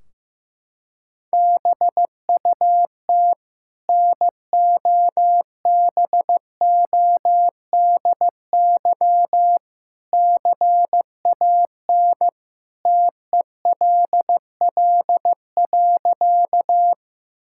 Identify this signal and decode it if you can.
Morse code, heard as a keyed tone, about 15 wpm, 700 Hz. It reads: BUT NOBODY CAN TELL.
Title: Sentences from Top 1600 Words 15wpm